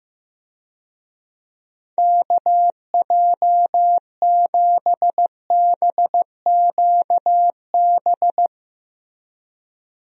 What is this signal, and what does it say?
Morse code, heard as a keyed tone, about 14 wpm, 700 Hz. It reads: KJ7BQB
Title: USA Call Signs - 6 Characters - Repeated 3x - 15wpm